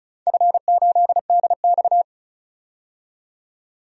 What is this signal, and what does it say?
Morse code, heard as a keyed tone, about 36 wpm, 700 Hz. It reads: F8DX
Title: DXCC-100 Call Signs - 4 Characters - 35wpm